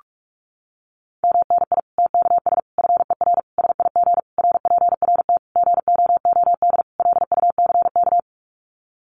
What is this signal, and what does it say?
Morse code, heard as a keyed tone, about 44 wpm, 700 Hz. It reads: MNI TKS FER SIG RPRT GOOD LUCK
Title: Common QSO Phrases 45wpm